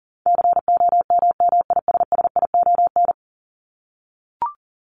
Morse code, heard as a keyed tone, about 40 wpm, 700 Hz. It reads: COMMISSION
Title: Top 1251-1500 Words 40wpm